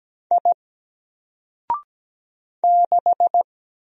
Morse code, heard as a keyed tone, about 17 wpm, 700 Hz. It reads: I 6
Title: Single Character (Letter:Number:Punctuation:Prosign) 17wpm